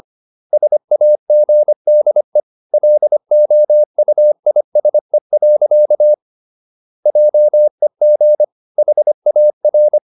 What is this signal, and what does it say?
Morse code, heard as a keyed tone, about 25 wpm, 600 Hz. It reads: SAGDE LOUISE. JEG HAR
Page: Ferie 25 wpm